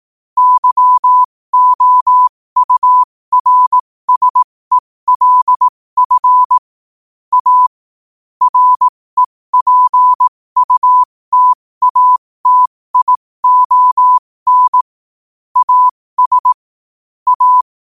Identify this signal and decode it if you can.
Morse code, heard as a keyed tone, about 18 wpm, spaced as for 15 wpm, 1000 Hz. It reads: YOURSELF A REPUTATION AS A
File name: qotdfiles / 15wpm